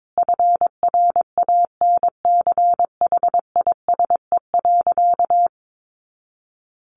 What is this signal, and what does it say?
Morse code, heard as a keyed tone, about 22 wpm, 700 Hz. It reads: FRANCHISE.
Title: A Sound Of Thunder 22wpm